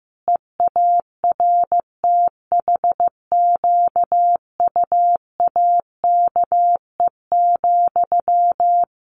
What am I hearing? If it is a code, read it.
Morse code, heard as a keyed tone, about 15 wpm, 700 Hz. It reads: EARTHQUAKE,
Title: A Sound Of Thunder 15wpm